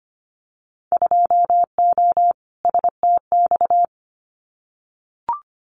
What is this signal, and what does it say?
Morse code, heard as a keyed tone, about 25 wpm, 700 Hz. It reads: 2O STX
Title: Winter Field Day Exchanges 25wpm